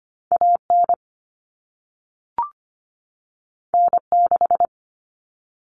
Morse code, heard as a keyed tone, about 25 wpm, 700 Hz. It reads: AN N6
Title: Common Cut Numbers - 2-Digit 25wpm